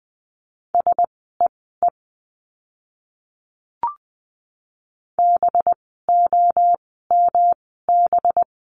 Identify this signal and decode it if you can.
Morse code, heard as a keyed tone, about 20 wpm, spaced as for 14 wpm, 700 Hz. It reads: SEE BOMB
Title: 185 Lesson 30f - Letter b - Words 20wpm